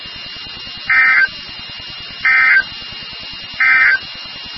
Emergency Alert System (SAME) tones, heard as harsh buzzing bursts. It is heard from 0.8 to 4.1 s.
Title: NOAA Weather Radio WXK36